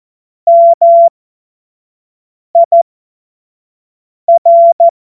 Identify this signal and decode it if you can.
Morse code, heard as a keyed tone, about 14 wpm, spaced as for 5 wpm, 680 Hz. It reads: MIR
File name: Morsekurs / Witze / Witz 19. Oktober 2021, 25 F70